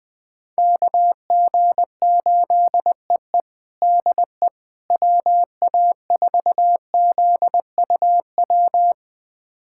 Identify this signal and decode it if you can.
Morse code, heard as a keyed tone, about 20 wpm, 700 Hz. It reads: KG8EE DE WA4ZUW
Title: USA Callsign DE Callsign 20wpm